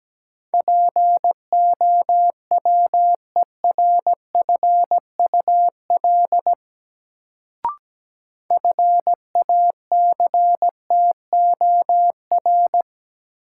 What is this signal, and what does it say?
Morse code, heard as a keyed tone, about 17 wpm, 700 Hz. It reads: POWERFUL FACTOR
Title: Sentences from Top 1300 Words - New Word Drill Alt - 17wpm